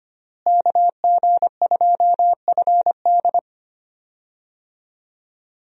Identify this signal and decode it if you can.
Morse code, heard as a keyed tone, about 25 wpm, 700 Hz. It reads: KG2FD
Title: USA Call Signs - 5 Characters - Repeated 2x - 25wpm